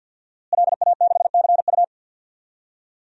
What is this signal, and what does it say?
Morse code, heard as a keyed tone, about 52 wpm, 700 Hz. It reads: LA6CU
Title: DXCC-100 Call Signs - 5 Characters - Repeated 3x - 50wpm